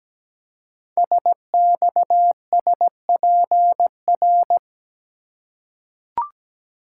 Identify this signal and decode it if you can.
Morse code, heard as a keyed tone, about 17 wpm, 700 Hz. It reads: SXSPR
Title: Code Groups - 5 Letters - 17wpm